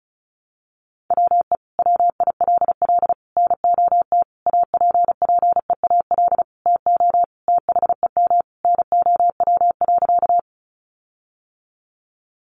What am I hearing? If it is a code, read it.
Morse code, heard as a keyed tone, about 34 wpm, 700 Hz. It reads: WE WILL NOT APPEAL TO THEM NOW.
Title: Sentences from Top 1300 Words 35wpm